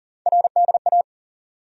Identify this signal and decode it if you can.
Morse code, heard as a keyed tone, about 40 wpm, 700 Hz. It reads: RDA
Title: USA Call Sign Postfixes - 3 Letters 40wpm